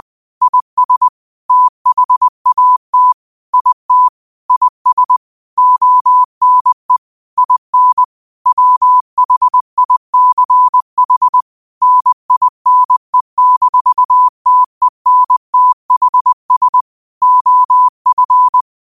Morse code, heard as a keyed tone, about 20 wpm, 1000 Hz. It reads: IS THAT IT IS ONE IN WHICH NINE-TENTHS OF
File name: qotdfiles / 20wpm